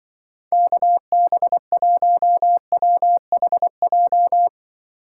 Morse code, heard as a keyed tone, about 24 wpm, 700 Hz. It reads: KB1WHJ
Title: Speed Racing - USA 2x3 Call Signs 20wpm